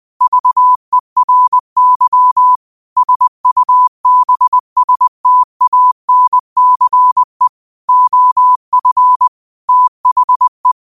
Morse code, heard as a keyed tone, about 20 wpm, 1000 Hz. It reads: VERY SUBSTANCE OF THE
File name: qotdfiles / 20wpm